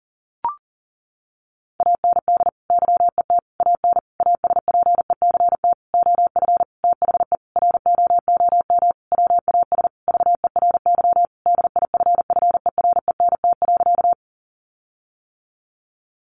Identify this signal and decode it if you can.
Morse code, heard as a keyed tone, about 40 wpm, 700 Hz. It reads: AND YET AN ASPECT OF THE ROOM WAS VERY DIFFERENT.
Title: Sentences from Top 1300 Words 40wpm